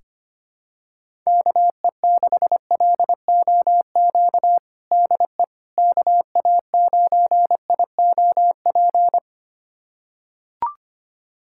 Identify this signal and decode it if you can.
Morse code, heard as a keyed tone, about 25 wpm, 700 Hz. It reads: KE6LOQ DE KA9IOP
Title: USA Callsign DE Callsign 25wpm